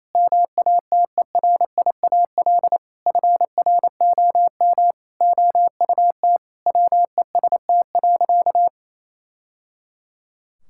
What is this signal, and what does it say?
Morse code, heard as a keyed tone, about 28 wpm, 700 Hz. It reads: MATERIAL FROM OUT WEST.
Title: Sentences from Top 1000 Words - Review Part 5 - 28wpm